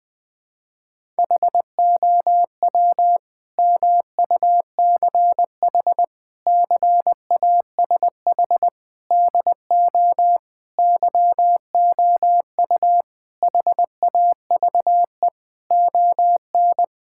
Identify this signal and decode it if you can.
Morse code, heard as a keyed tone, about 20 wpm, 700 Hz. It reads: HOW MUCH CASH DO YOU HAVE ON
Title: Sentences from Top 1000 Words - Review Part 5 - 20wpm